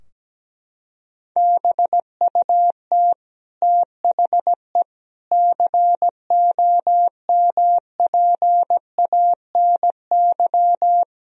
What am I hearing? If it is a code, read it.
Morse code, heard as a keyed tone, about 17 wpm, 700 Hz. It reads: BUT THE COMPANY
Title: Sentences from Top 1000 Words - Review Part 3 - 17wpm